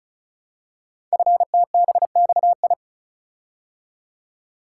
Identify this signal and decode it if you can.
Morse code, heard as a keyed tone, about 36 wpm, 690 Hz. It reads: FTBXI
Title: Code Groups - 5 Letters - 35wpm